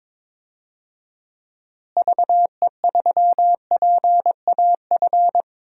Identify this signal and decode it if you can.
Morse code, heard as a keyed tone, about 22 wpm, 700 Hz. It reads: VE3PAF
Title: DXCC-100 Call Signs - 6 Characters - Repeated 3x - 22wpm